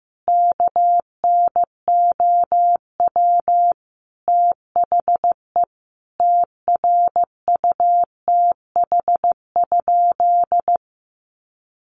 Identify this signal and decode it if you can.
Morse code, heard as a keyed tone, about 15 wpm, 700 Hz. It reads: KNOW THE TRUTH?
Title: Sentences from Top 800 Words 15wpm